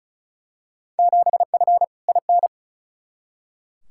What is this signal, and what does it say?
Morse code, heard as a keyed tone, about 36 wpm, 690 Hz. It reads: 7F IN
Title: Field Day Exchanges Encore - 35wpm